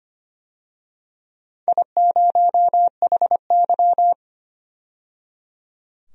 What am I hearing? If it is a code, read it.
Morse code, heard as a keyed tone, about 25 wpm, 700 Hz. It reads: I0HY
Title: DXCC-100 Call Signs - 4 Characters - 25wpm